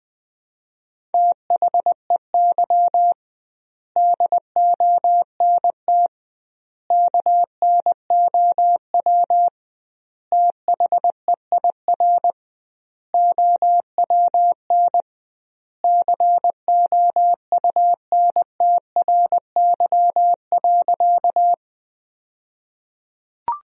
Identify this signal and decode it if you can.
Morse code, heard as a keyed tone, about 20 wpm, 700 Hz. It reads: THEY DONT KNOW THEIR OWN COUNTRY.
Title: 2x Word-Spacing - Sentences from Top 300 Words - 20wpm